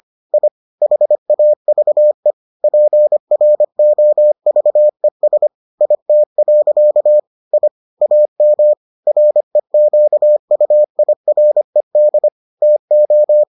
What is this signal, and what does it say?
Morse code, heard as a keyed tone, about 25 wpm, 600 Hz. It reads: I HAVE PROVES IT. I AM REQUIRED TO